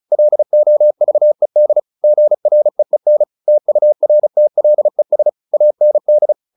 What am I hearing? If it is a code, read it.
Morse code, heard as a keyed tone, about 34 wpm, 600 Hz. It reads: LOVED GREEN TURTLES AND
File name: the old man and the sea / the old man and the sea 1 35wpm